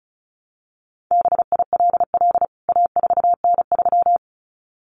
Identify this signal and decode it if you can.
Morse code, heard as a keyed tone, about 34 wpm, 700 Hz. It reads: BILL A4N3
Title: CWT Exchange 35wpm